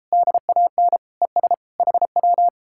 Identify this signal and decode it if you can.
Morse code, heard as a keyed tone, about 38 wpm, 700 Hz. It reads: DAN ES HW